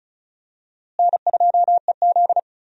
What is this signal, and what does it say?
Morse code, heard as a keyed tone, about 34 wpm, 700 Hz. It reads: N2EZ
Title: USA Call Signs - 4 Characters - Repeated 3x - 35wpm